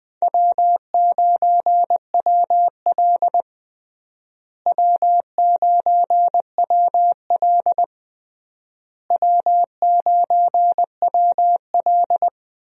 Morse code, heard as a keyed tone, about 20 wpm, 700 Hz. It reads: W9WL W9WL W9WL
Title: USA Call Signs - 4 Characters - Repeated 3x - 20wpm